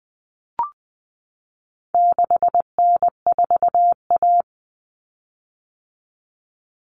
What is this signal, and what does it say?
Morse code, heard as a keyed tone, about 20 wpm, 700 Hz. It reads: 6N4A
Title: Common Cut Numbers - 4-Digit 20wpm